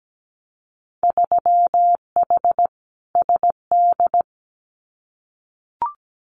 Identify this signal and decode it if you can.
Morse code, heard as a keyed tone, about 17 wpm, 700 Hz. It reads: 3H SD
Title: Winter Field Day Exchanges Encore 17wpm